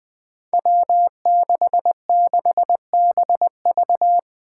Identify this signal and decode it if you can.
Morse code, heard as a keyed tone, about 20 wpm, 700 Hz. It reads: W66BV
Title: Five Challenging Letter-Number Combinations 20wpm